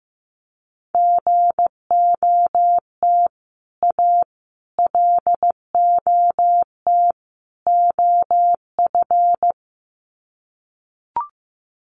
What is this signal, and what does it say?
Morse code, heard as a keyed tone, about 15 wpm, 700 Hz. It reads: GOT A LOT OF
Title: Sets of 4 Words 15wpm